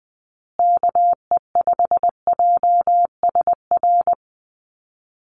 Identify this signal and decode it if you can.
Morse code, heard as a keyed tone, about 20 wpm, 700 Hz. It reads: KE5JSR
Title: USA Call Signs Repeated 3x 20wpm